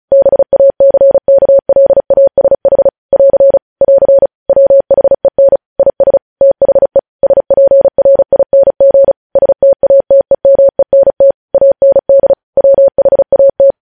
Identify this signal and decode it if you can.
Morse code, heard as a keyed tone, about 36 wpm, 550 Hz. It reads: BACKLASH <AR> <AR> WHEN IS THE SPRING STATEMENT AND WHAT
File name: podcast / News headlines-1x-35x35